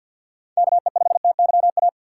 Morse code, heard as a keyed tone, about 47 wpm, 700 Hz. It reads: KE5TYA
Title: USA Call Signs - 6 Characters - 6x Character Spacing - 50wpm